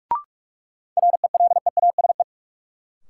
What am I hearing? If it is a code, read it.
Morse code, heard as a keyed tone, about 47 wpm, 700 Hz. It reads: RELEASE
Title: Sentences from Top 1100 Words - New Word Drill - 45wpm